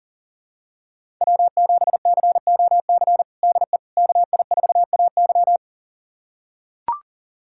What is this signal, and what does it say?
Morse code, heard as a keyed tone, about 40 wpm, 700 Hz. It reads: W7COC DE KI4AY
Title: USA Callsign DE Callsign - 5 Characters - 40wpm